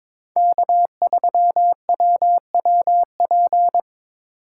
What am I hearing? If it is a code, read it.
Morse code, heard as a keyed tone, about 22 wpm, 700 Hz. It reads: K3WWP
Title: K1USN SST QSO Practice - 22wpm